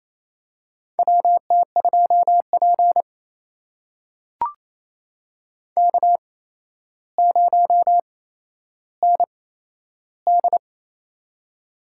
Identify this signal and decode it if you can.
Morse code, heard as a keyed tone, about 28 wpm, spaced as for 12 wpm, 700 Hz. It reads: WT2P K0ND
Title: USA Call Signs - 4 Characters - 8x Character Spacing - 28wpm